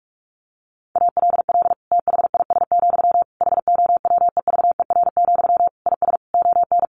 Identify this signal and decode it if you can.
Morse code, heard as a keyed tone, about 44 wpm, 700 Hz. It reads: ALL THIS, HOWEVER, IS ON
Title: Sentences from Top 600 Words 45wpm